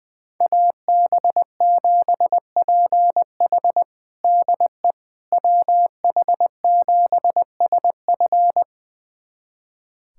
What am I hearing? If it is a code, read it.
Morse code, heard as a keyed tone, about 21 wpm, 700 Hz. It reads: AB7PH DE WH7SF
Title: USA Callsign DE Callsign - 5 Characters - 20wpm